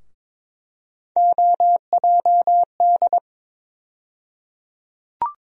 Morse code, heard as a keyed tone, about 23 wpm, 700 Hz. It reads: OJD
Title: USA Call Sign Postfixes - 3 Letters 22wpm